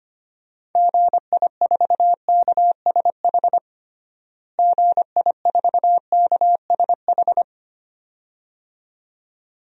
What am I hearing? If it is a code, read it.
Morse code, heard as a keyed tone, about 25 wpm, 700 Hz. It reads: GI4KSH GI4KSH
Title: DXCC-100 Call Signs - 6 Characters - Repeated 2x - 25wpm